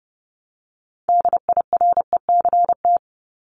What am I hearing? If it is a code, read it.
Morse code, heard as a keyed tone, about 30 wpm, 700 Hz. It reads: DIRECT
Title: Top 500 Words Repeated 30wpm